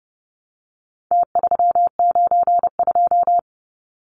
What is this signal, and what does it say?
Morse code, heard as a keyed tone, about 30 wpm, 700 Hz. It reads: T392
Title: Numbers - Contest Style - 4-Digit 30wpm